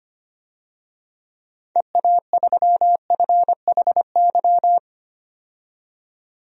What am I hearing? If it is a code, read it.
Morse code, heard as a keyed tone, about 25 wpm, 700 Hz. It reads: EA3FHY
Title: DXCC-100 Call Signs - 6 Characters - 25wpm